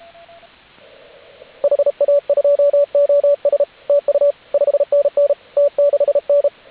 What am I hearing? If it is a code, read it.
Morse code, heard as a keyed tone, about 33 wpm, 570 Hz, with radio noise. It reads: HA2OS TU 5NN T6N
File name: AUDIO2019 / HG8DX CW